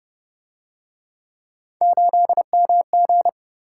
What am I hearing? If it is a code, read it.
Morse code, heard as a keyed tone, about 30 wpm, 700 Hz. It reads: 8MG